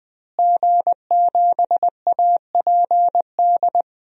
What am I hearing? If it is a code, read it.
Morse code, heard as a keyed tone, about 20 wpm, 700 Hz. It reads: G7APD
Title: DXCC-100 Call Signs - 5 Characters - Repeated 3x - 20wpm